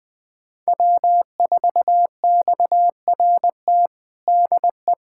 Morse code, heard as a keyed tone, about 20 wpm, 700 Hz. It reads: W4XRT DE
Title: USA Callsign DE Callsign 20wpm